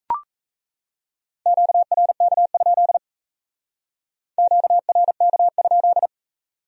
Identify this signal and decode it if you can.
Morse code, heard as a keyed tone, about 35 wpm, 700 Hz. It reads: QRK? QRK?
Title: Speed-Racing Q-Codes - 28wpm